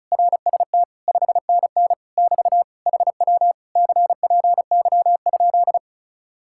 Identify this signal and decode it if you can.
Morse code, heard as a keyed tone, about 40 wpm, 700 Hz. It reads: RST 5NN <BT> HW CPY?